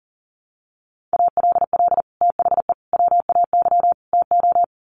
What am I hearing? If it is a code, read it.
Morse code, heard as a keyed tone, about 40 wpm, 700 Hz. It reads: ALL THE WAY TO
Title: Sets of 4 Words 40wpm